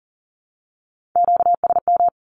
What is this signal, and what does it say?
Morse code, heard as a keyed tone, about 40 wpm, 700 Hz. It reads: QSM
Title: Q-Codes Repeated 3x 40wpm